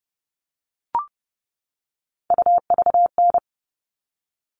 Morse code, heard as a keyed tone, about 30 wpm, 700 Hz. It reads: UVN
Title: All Cut Numbers - 3-Digit 30wpm